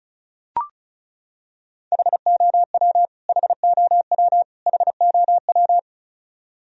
Morse code, heard as a keyed tone, about 34 wpm, 700 Hz. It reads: HOW HOW HOW
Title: Top 100 Words Repeated 35wpm